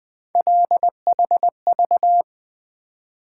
Morse code, heard as a keyed tone, about 20 wpm, 700 Hz. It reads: LHV
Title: Three Letter-Number Combinations - 20wpm